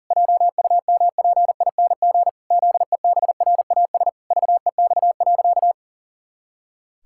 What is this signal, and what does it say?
Morse code, heard as a keyed tone, about 40 wpm, 700 Hz. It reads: JUMPING ZEBRAS VEX.
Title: Pangram Challenge v2 - 40wpm